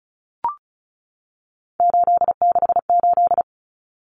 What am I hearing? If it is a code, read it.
Morse code, heard as a keyed tone, about 34 wpm, 700 Hz. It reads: 868
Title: Numbers - 3-Digit 35wpm